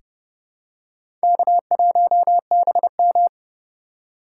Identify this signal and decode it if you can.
Morse code, heard as a keyed tone, about 28 wpm, 700 Hz. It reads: K1BM
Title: USA Call Signs - 4 Characters - 8x Character Spacing - 30wpm